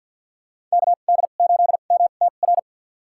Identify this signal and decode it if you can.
Morse code, heard as a keyed tone, about 53 wpm, spaced as for 37 wpm, 700 Hz. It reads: KD8MTR
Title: USA Call Signs - 2x Character Spacing - 50wpm